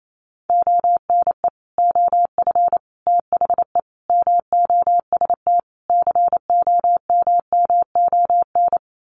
Morse code, heard as a keyed tone, about 28 wpm, 700 Hz. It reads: ONE OF THE MOST COMMON
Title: Sets of 5 Words Encore2 28wpm